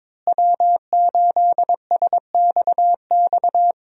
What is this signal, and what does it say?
Morse code, heard as a keyed tone, about 22 wpm, 700 Hz. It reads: W8SXX
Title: USA Call Signs - 5 Characters - Repeated 3x - 22wpm